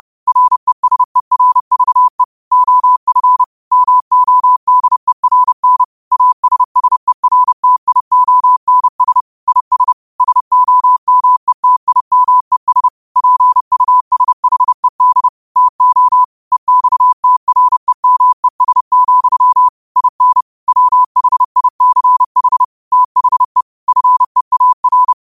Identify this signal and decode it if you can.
Morse code, heard as a keyed tone, about 30 wpm, 1000 Hz. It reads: RESERVE OF MODERN ASSERTIONS IS SOMETIMES PUSHED TO EXTREMES, IN WHICH THE FEAR
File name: qotdfiles / 30wpm